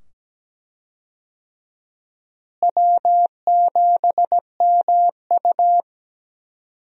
Morse code, heard as a keyed tone, about 17 wpm, 700 Hz. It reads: W7MU
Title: USA Call Signs - 4 Characters - Repeated 2x - 17wpm